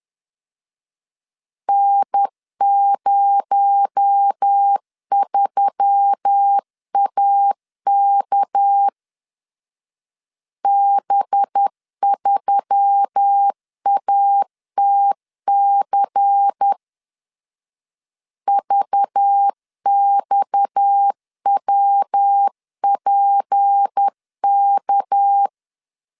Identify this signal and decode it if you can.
Morse code, heard as a keyed tone, about 10 wpm, 800 Hz. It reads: N03AK B3ATC VXWPK
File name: CW / DET / 1 6 1500C S B 10wpm 4137